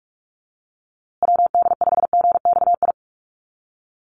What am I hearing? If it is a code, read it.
Morse code, heard as a keyed tone, about 44 wpm, 700 Hz. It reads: WD5GXI
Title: USA Call Signs Repeated 2x 45wpm